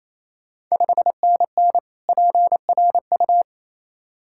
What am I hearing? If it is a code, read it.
Morse code, heard as a keyed tone, about 27 wpm, 690 Hz. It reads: 5NN PRU
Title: Ontario QSO Party Exchange 28wpm